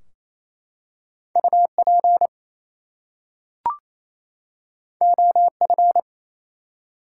Morse code, heard as a keyed tone, about 27 wpm, 700 Hz. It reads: UP OF
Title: Two Letter Words - 28wpm